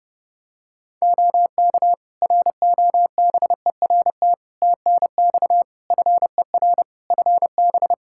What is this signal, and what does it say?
Morse code, heard as a keyed tone, about 32 wpm, 700 Hz. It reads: OK ROBERT TNX FER FB
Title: K1USN SST QSO Practice 30wpm